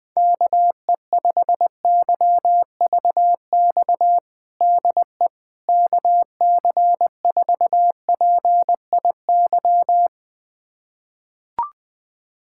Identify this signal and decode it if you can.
Morse code, heard as a keyed tone, about 20 wpm, 700 Hz. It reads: KE5YVX DE KC4PIY
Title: USA Callsign DE Callsign - 6 Characters - 20wpm